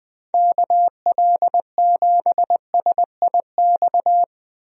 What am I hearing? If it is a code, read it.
Morse code, heard as a keyed tone, about 20 wpm, 700 Hz. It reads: KL7SIX
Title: DXCC-100 Call Signs - 6 Characters - 20wpm